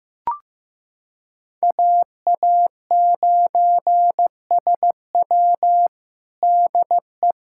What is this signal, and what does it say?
Morse code, heard as a keyed tone, about 15 wpm, 700 Hz. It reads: AA9SW DE
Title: USA Callsign DE Callsign 15wpm